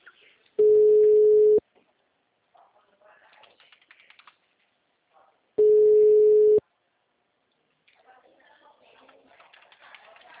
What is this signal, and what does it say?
Ringback tone (Europe, 425 Hz), with speech.